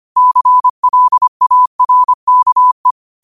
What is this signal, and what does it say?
Morse code, heard as a keyed tone, about 25 wpm, 1000 Hz. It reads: CLARKE